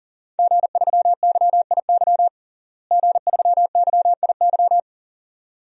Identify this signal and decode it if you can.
Morse code, heard as a keyed tone, about 40 wpm, 700 Hz. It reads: G3YIY G3YIY
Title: DXCC-100 Call Signs - 5 Characters - Repeated 2x - 40wpm